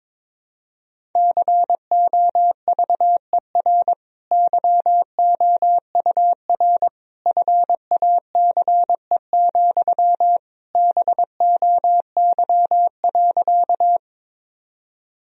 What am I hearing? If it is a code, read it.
Morse code, heard as a keyed tone, about 22 wpm, 700 Hz. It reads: COVER YOUR FACE, BOY.
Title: Sentences from Top 1000 Words - Review Part 1 - 22wpm